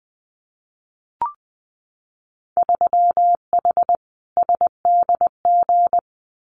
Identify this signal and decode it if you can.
Morse code, heard as a keyed tone, about 21 wpm, 700 Hz. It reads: 3H SDG
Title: Winter Field Day Exchanges Encore 20wpm